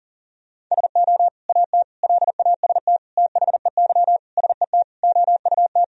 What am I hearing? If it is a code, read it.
Morse code, heard as a keyed tone, about 40 wpm, 700 Hz. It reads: SO AT LAST THEY SET OUT
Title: Sentences from Top 1000 Words - Review Part 1 - 40wpm